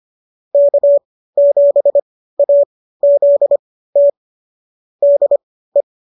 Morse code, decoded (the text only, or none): K7AZT DE